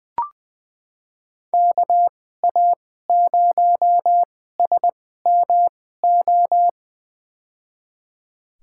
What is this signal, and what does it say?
Morse code, heard as a keyed tone, about 20 wpm, spaced as for 14 wpm, 700 Hz. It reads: KA0SMO
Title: USA Call Signs - 6 Characters - 2x Character Spacing - 20wpm